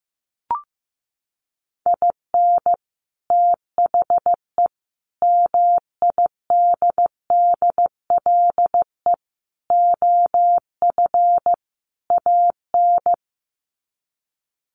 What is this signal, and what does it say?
Morse code, heard as a keyed tone, about 15 wpm, 700 Hz. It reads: IN THE MIDDLE OF AN
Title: Sets of 5 Words Encore2 15wpm